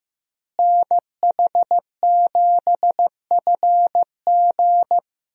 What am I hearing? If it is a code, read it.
Morse code, heard as a keyed tone, about 15 wpm, 700 Hz. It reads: NH7FG
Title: USA Call Signs - 5 Characters - Repeated 2x - 15wpm